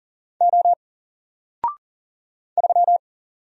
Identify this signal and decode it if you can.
Morse code, heard as a keyed tone, about 40 wpm, 700 Hz. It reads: O 3
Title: Single Letter-Number Alt 40wpm